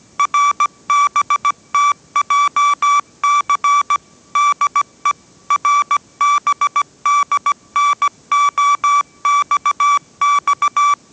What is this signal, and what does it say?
Morse code, heard as a keyed tone, about 17 wpm, 1160 Hz. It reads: RBTJC DE RBDNOXX
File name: 18wpm